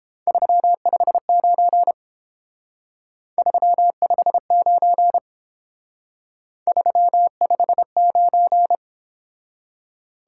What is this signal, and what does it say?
Morse code, heard as a keyed tone, about 29 wpm, 700 Hz. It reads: 359 359 359